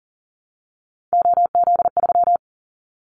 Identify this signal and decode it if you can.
Morse code, heard as a keyed tone, about 40 wpm, 700 Hz. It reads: OZ3
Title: Callsign Prefixes for Top 50 DXCC - 40wpm